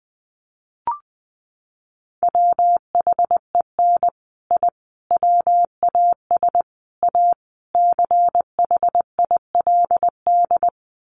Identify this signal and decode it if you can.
Morse code, heard as a keyed tone, about 19 wpm, 700 Hz. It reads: WHEN I WAS A CHILD
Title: Sets of 5 Words Encore 20wpm